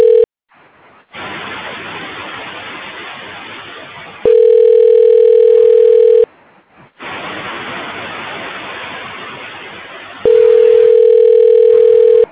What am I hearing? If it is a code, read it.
Ringback tone (North America), 440 + 480 Hz.